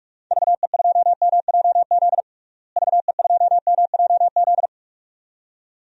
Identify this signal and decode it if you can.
Morse code, heard as a keyed tone, about 41 wpm, 700 Hz. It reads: VE2MJZ VE2MJZ
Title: DXCC-100 Call Signs - 6 Characters - Repeated 2x - 45wpm